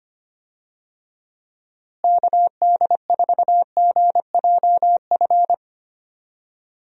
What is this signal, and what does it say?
Morse code, heard as a keyed tone, about 25 wpm, 700 Hz. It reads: KD4GJF
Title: USA Call Signs - 6 Characters - Repeated 3x - 25wpm